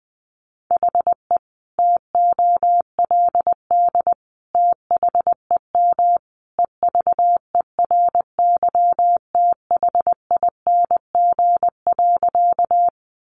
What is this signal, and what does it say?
Morse code, heard as a keyed tone, about 20 wpm, 700 Hz. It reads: HE TOLD THEM EVERYTHING.
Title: Sentences from Top 700 Words 20wpm